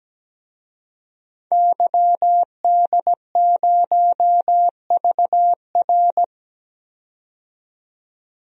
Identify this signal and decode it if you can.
Morse code, heard as a keyed tone, about 17 wpm, 700 Hz. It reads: YD0VR
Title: DXCC-100 Call Signs - 5 Characters - 17wpm